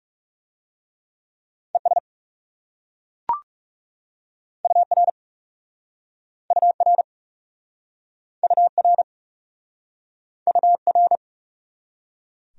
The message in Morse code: ES UR UR UR UR

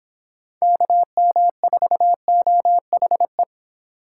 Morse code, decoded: KM4OHE

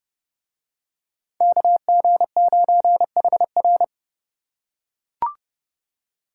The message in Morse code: KG9HR

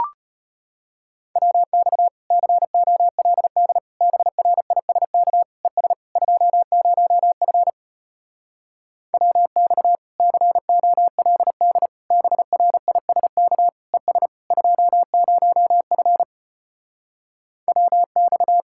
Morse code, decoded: WX COLD BRISK ES 20F WX COLD BRISK ES 20F WX

